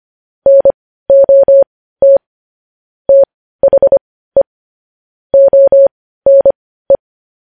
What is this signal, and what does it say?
Morse code, heard as a keyed tone, about 25 wpm, spaced as for 15 wpm, 550 Hz. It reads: NOT THE ONE